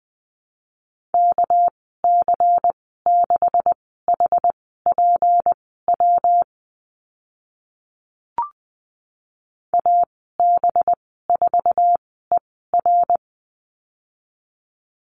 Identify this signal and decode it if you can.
Morse code, heard as a keyed tone, about 20 wpm, spaced as for 14 wpm, 700 Hz. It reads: KC6HPW AB4ER
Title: 245 Lesson 39g - Letter z - Call Signs 20wpm